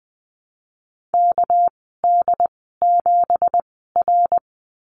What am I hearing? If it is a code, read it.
Morse code, heard as a keyed tone, about 20 wpm, spaced as for 14 wpm, 700 Hz. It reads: KD7R
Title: 207 Lesson 33g - Letter j - Call Signs 20wpm